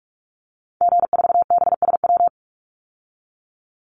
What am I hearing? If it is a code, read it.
Morse code, heard as a keyed tone, about 44 wpm, 700 Hz. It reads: G4BSW
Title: Call Signs Repeated 3x - 45wpm